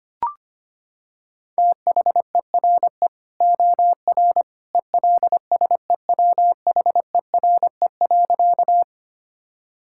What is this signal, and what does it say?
Morse code, heard as a keyed tone, about 25 wpm, 700 Hz. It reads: THERE OR ELSEWHERE.
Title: Sentences from Top 1600 Words 25wpm